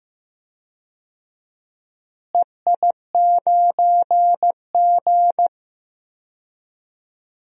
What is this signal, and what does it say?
Morse code, heard as a keyed tone, about 15 wpm, 700 Hz. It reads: EI9G
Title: DXCC-100 Call Signs - 4 Characters - Repeated 2x - 15wpm